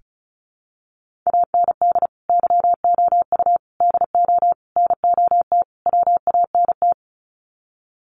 Morse code, heard as a keyed tone, about 34 wpm, 700 Hz. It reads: AND YOU DO NOT WANT
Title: Sets of 5 Words Encore2 35wpm